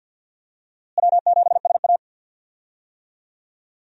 Morse code, heard as a keyed tone, about 52 wpm, 700 Hz. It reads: W7SA
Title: USA Call Signs - 4 Characters - Repeated 2x - 50wpm